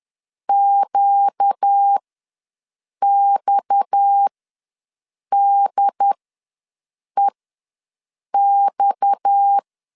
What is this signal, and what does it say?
Morse code, heard as a keyed tone, about 10 wpm, spaced as for 6 wpm, 800 Hz. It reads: QXDEX
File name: LEZIONI / 6 PKX0 / 6 PKX ws 10wpm 3706